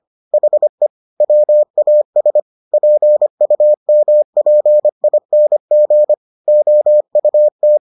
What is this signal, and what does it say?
Morse code, heard as a keyed tone, about 25 wpm, 600 Hz. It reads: HE WAS PUMPING OUT